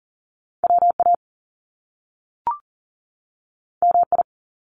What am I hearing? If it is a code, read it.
Morse code, heard as a keyed tone, about 42 wpm, 700 Hz. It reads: WA MI
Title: US States 40wpm